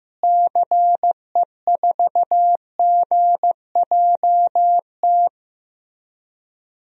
Morse code, heard as a keyed tone, about 15 wpm, 700 Hz. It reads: CE4GJT
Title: DXCC-100 Call Signs - 6 Characters - Repeated 3x - 15wpm